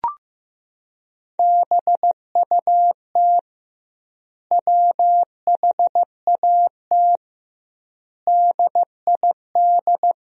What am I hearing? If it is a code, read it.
Morse code, heard as a keyed tone, about 15 wpm, 700 Hz. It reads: BUT WHAT DID
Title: 2x Word-Spacing - Sentences from Top 100 Words - 15wpm